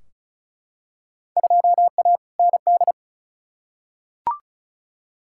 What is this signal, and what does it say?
Morse code, heard as a keyed tone, about 34 wpm, 700 Hz. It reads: 2A ND
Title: Field Day Exchanges Encore - 35wpm